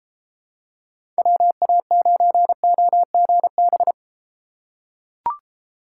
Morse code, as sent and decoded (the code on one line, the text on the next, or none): .-- .- ----. --- --. -...
WA9OGB